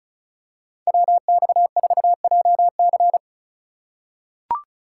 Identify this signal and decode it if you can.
Morse code, heard as a keyed tone, about 36 wpm, 700 Hz. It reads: WX4JC
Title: USA Call Signs - 5 Characters - 6x Character Spacing - 35wpm